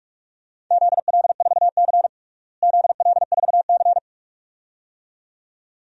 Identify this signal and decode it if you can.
Morse code, heard as a keyed tone, about 47 wpm, 700 Hz. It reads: ZL4C ZL4C
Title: DXCC-100 Call Signs - 4 Characters - Repeated 2x - 45wpm